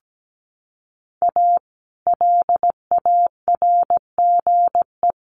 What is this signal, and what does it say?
Morse code, heard as a keyed tone, about 17 wpm, 700 Hz. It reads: A LARGE
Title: Sets of 4 Words 17wpm